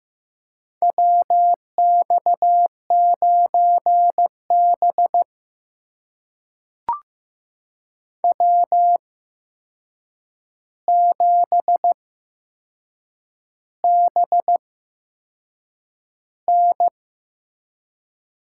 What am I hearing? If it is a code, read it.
Morse code, heard as a keyed tone, about 15 wpm, spaced as for 6 wpm, 700 Hz. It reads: WX9B W7BN
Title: USA Call Signs - 4 Characters - 8x Character Spacing - 15wpm